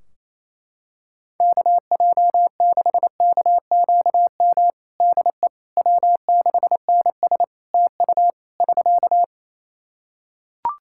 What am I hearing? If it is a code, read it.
Morse code, heard as a keyed tone, about 28 wpm, 700 Hz. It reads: KJ6KQM DE W6NS TU <SK>
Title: Practice QSO - Exchange 4 - Ending - 28wpm